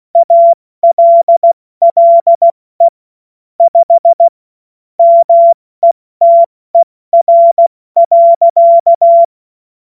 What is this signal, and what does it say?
Morse code, heard as a keyed tone, about 16 wpm, 680 Hz. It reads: ALLE 5 METER.
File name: Morsekurs / Witze / Witz 22. August 2023 70 F80